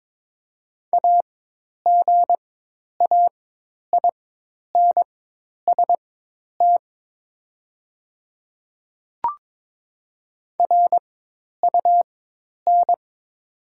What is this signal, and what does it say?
Morse code, heard as a keyed tone, about 22 wpm, spaced as for 10 wpm, 700 Hz. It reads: AGAINST RUN
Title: 4x Character Spacing - Top 500 Words - 22wpm